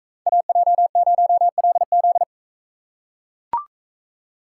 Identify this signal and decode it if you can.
Morse code, heard as a keyed tone, about 41 wpm, 700 Hz. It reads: AJ0LZ